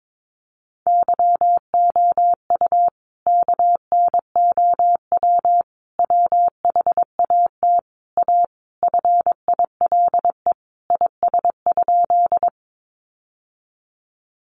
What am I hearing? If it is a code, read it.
Morse code, heard as a keyed tone, about 22 wpm, 700 Hz. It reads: YOU KNOW WHAT A FILE IS?
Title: Sentences from Top 1200 Words 22wpm